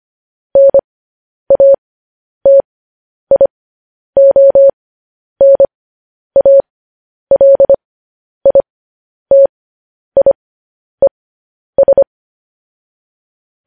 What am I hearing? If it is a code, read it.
Morse code, heard as a keyed tone, about 25 wpm, spaced as for 10 wpm, 550 Hz. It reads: NATIONALITIES
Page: News Headlines in Morse Code at 10 WPM